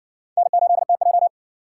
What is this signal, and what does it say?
Morse code, heard as a keyed tone, about 54 wpm, 700 Hz. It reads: N8TJ